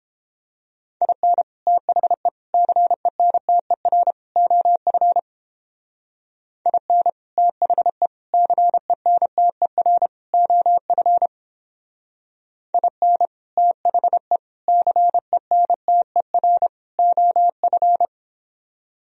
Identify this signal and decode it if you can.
Morse code, heard as a keyed tone, about 29 wpm, 700 Hz. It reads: IN THE CENTER OF IN THE CENTER OF IN THE CENTER OF